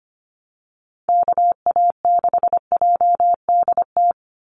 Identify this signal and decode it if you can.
Morse code, heard as a keyed tone, about 25 wpm, 700 Hz. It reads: KA6JDT
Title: USA Call Signs 25wpm